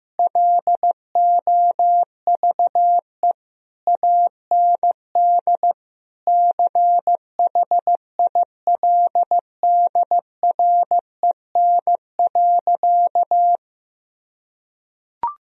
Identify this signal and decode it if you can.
Morse code, heard as a keyed tone, about 15 wpm, 700 Hz. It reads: LOVE AND CHILDREN.
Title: Sentences from Top 1000 Words - Review Part 3 - 15wpm